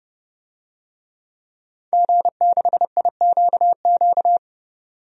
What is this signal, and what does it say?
Morse code, heard as a keyed tone, about 30 wpm, 700 Hz. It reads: G6IQQ
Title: DXCC-100 Call Signs - 5 Characters - 30wpm